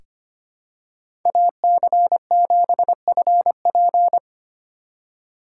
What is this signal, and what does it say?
Morse code, heard as a keyed tone, about 25 wpm, 700 Hz. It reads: AC7FP
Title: USA Call Signs - 5 Characters - 2x Character Spacing - 25wpm